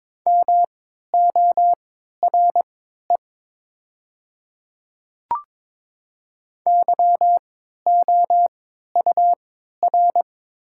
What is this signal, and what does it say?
Morse code, heard as a keyed tone, about 22 wpm, spaced as for 12 wpm, 700 Hz. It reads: MORE YOUR
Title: 3x Character Spacing - Top 100 Words - 22wpm